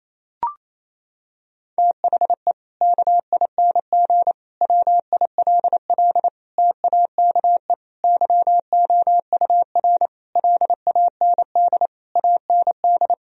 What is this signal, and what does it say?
Morse code, heard as a keyed tone, about 28 wpm, 700 Hz. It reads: THE KING WILL TAKE YOUR LAND AND
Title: Sentences from Top 1000 Words - Review Part 4 - 28wpm